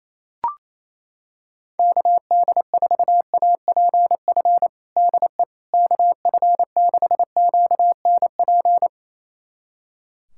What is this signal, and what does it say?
Morse code, heard as a keyed tone, about 27 wpm, 700 Hz. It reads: KD4APF DE KF6QNP
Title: USA Callsign DE Callsign - 6 Characters - 28wpm